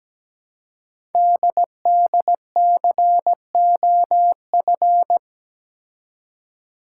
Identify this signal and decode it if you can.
Morse code, heard as a keyed tone, about 17 wpm, 700 Hz. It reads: DDCOF
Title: Code Groups - 5 Letters - 17wpm